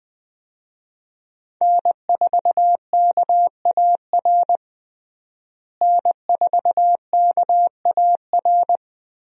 Morse code, heard as a keyed tone, about 21 wpm, 700 Hz. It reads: N4KAR N4KAR
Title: USA Call Signs - 5 Characters - Repeated 2x - 20wpm